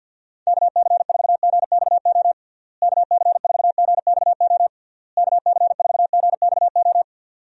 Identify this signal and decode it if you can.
Morse code, heard as a keyed tone, about 53 wpm, 700 Hz. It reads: KC4GXO KC4GXO KC4GXO
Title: USA Call Signs - 6 Characters - Repeated 3x - 50wpm